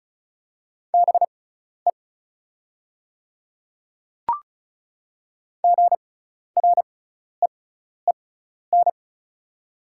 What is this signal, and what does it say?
Morse code, heard as a keyed tone, about 36 wpm, spaced as for 12 wpm, 690 Hz. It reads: BE GREEN
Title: 6x Character Spacing - Top 500 Words - 35wpm